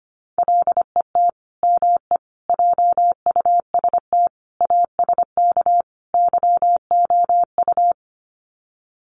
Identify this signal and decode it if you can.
Morse code, heard as a keyed tone, about 25 wpm, 700 Hz. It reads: LET ME JUST ASK YOU
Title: Sets of 5 Words Encore2 25wpm